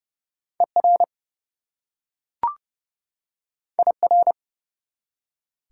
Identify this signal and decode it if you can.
Morse code, heard as a keyed tone, about 30 wpm, 700 Hz. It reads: ER IR
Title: Prefixes and Suffixes 30wpm